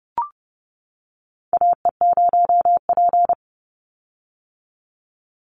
Morse code, heard as a keyed tone, about 30 wpm, 700 Hz. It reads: AE0P
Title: Call Signs - 30wpm